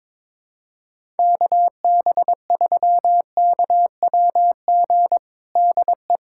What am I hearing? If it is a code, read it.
Morse code, heard as a keyed tone, about 22 wpm, 700 Hz. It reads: KB3KWG DE
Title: USA Callsign DE Callsign 22wpm